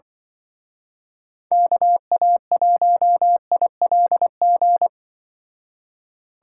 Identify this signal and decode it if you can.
Morse code, heard as a keyed tone, about 24 wpm, 700 Hz. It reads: KA1ILG
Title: Speed Racing - USA 2x3 Call Signs 20wpm